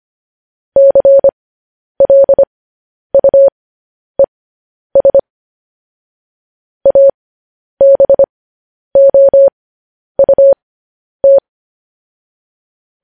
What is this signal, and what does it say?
Morse code, heard as a keyed tone, about 25 wpm, spaced as for 10 wpm, 550 Hz. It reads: CLUES ABOUT
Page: News Headlines in Morse Code at 10 WPM